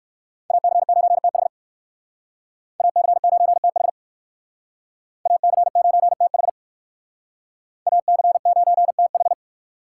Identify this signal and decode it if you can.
Morse code, heard as a keyed tone, about 57 wpm, 700 Hz. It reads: AC9TH AC9TH AC9TH AC9TH